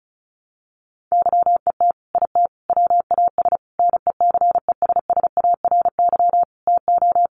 Morse code, heard as a keyed tone, about 34 wpm, 700 Hz. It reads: YET IT WAS NECESSARY TO
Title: Sentences from Top 800 Words 35wpm